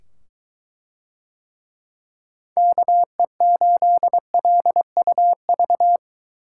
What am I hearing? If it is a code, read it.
Morse code, heard as a keyed tone, about 22 wpm, 700 Hz. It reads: KE8LUV